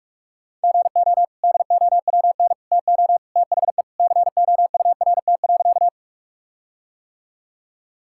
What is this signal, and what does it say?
Morse code, heard as a keyed tone, about 43 wpm, 700 Hz. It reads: GO DOWN TO THE COURT.
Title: Sentences from Top 1600 Words 45wpm